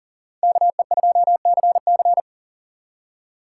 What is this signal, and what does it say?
Morse code, heard as a keyed tone, about 52 wpm, 700 Hz. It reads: KE2CC